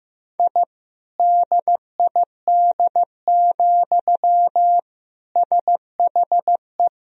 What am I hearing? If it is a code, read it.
Morse code, heard as a keyed tone, about 16 wpm, 700 Hz. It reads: I DID, SHE
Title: Sentences from Top 1000 Words - Review Part 4 - 15wpm